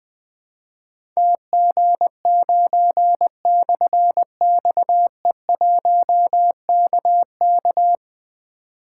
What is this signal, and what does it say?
Morse code, heard as a keyed tone, about 21 wpm, 700 Hz. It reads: TG9/XE1KK
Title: Difficult Call Signs 20wpm